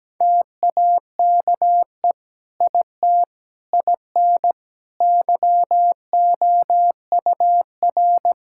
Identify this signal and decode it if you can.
Morse code, heard as a keyed tone, about 17 wpm, 700 Hz. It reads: TAKE IT IN YOUR
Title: Sentences from Top 1600 Words 17wpm